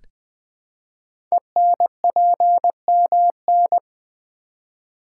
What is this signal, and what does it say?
Morse code, heard as a keyed tone, about 20 wpm, 700 Hz. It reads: ENPMN
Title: Code Groups - 5 Letter:Numbers - 20wpm